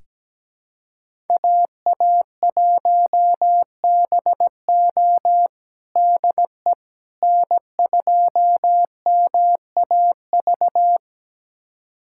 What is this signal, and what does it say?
Morse code, heard as a keyed tone, about 17 wpm, 700 Hz. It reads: AA1BO DE N2MAV
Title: USA Callsign DE Callsign - 5 Characters - 17wpm